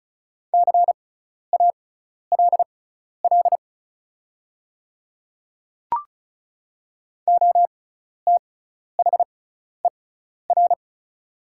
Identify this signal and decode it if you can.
Morse code, heard as a keyed tone, about 36 wpm, spaced as for 12 wpm, 700 Hz. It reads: CALL OTHER
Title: 6x Character Spacing - Top 200 Words - 35wpm